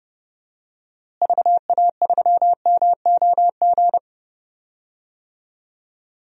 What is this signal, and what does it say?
Morse code, heard as a keyed tone, about 32 wpm, 700 Hz. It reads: VA3MOG
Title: DXCC-100 Call Signs - 6 Characters - Repeated 2x - 30wpm